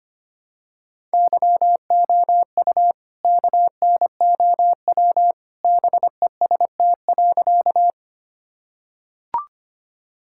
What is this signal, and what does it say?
Morse code, heard as a keyed tone, about 25 wpm, 700 Hz. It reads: YOU KNOW BEST.
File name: practice / 25wpm / Sentences with 2-Word Phrases - 25wpm